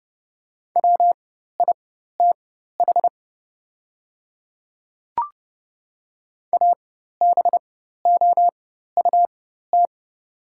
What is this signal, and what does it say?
Morse code, heard as a keyed tone, about 30 wpm, spaced as for 14 wpm, 700 Hz. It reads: WITH ABOUT
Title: 4x Character Spacing - Top 100 Words - 30wpm